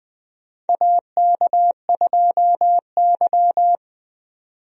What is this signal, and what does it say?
Morse code, heard as a keyed tone, about 20 wpm, 700 Hz. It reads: AK2Y
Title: USA Call Signs - 4 Characters - 20wpm